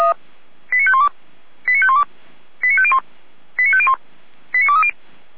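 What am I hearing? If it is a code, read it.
Five-tone selective call (ZVEI), heard as quick runs of five tones, tones of about 70 ms. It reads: 89512 89512 89501 89501 89230
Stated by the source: ZVEI five-tone selective call received by SDR.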